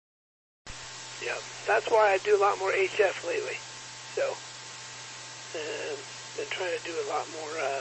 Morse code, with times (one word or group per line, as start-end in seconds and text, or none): none